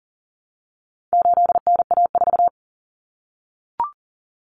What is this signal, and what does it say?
Morse code, heard as a keyed tone, about 40 wpm, 700 Hz. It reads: 8NA4
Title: Common Cut Numbers - 4-Digit 40wpm